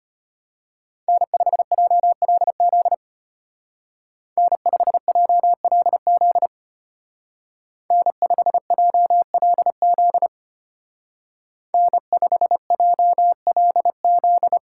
Morse code, heard as a keyed tone, about 33 wpm, 700 Hz. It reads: N5JLZ N5JLZ N5JLZ N5JLZ